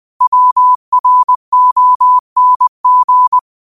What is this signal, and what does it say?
Morse code, heard as a keyed tone, about 20 wpm, 1000 Hz. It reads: WRONG
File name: qotdfiles / 20wpm